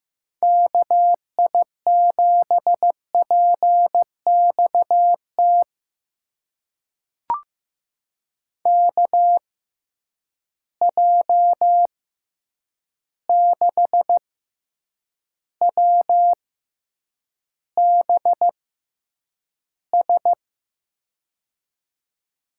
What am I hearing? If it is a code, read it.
Morse code, heard as a keyed tone, about 15 wpm, spaced as for 8 wpm, 700 Hz. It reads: KI7PXT KJ6WBS
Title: USA Call Signs - 6 Characters - 6x Character Spacing - 15wpm